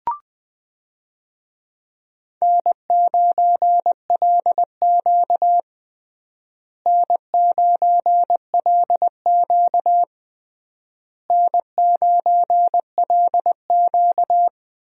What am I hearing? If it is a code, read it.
Morse code, heard as a keyed tone, about 19 wpm, 700 Hz. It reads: N9LQ N9LQ N9LQ
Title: USA Call Signs - 4 Characters - Repeated 3x - 20wpm